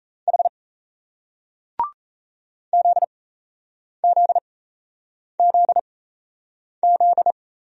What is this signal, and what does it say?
Morse code, heard as a keyed tone, about 38 wpm, spaced as for 8 wpm, 700 Hz. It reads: H ZZZZ